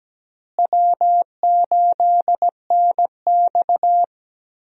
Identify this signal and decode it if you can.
Morse code, heard as a keyed tone, about 17 wpm, 700 Hz. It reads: W8NX
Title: USA Call Signs - 4 Characters - Repeated 3x - 17wpm